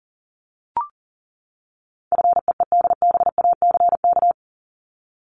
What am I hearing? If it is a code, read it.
Morse code, heard as a keyed tone, about 40 wpm, 700 Hz. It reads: FEEDBACK
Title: Top 1501-1750 Words 40wpm